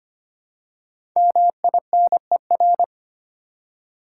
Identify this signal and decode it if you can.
Morse code, heard as a keyed tone, about 25 wpm, 700 Hz. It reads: MINER
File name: 5 Letter Words - First 3 Letters Given - 25wpm